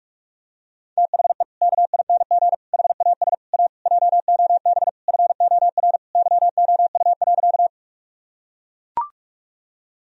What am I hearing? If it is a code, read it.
Morse code, heard as a keyed tone, about 41 wpm, 700 Hz. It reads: THE KING HAS A JOB FOR YOU.
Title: Sentences from Top 1000 Words - Review Part 3 - 45wpm